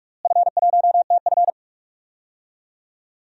Morse code, heard as a keyed tone, about 46 wpm, 700 Hz. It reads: F1TF
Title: DXCC-100 Call Signs - 4 Characters - Repeated 2x - 45wpm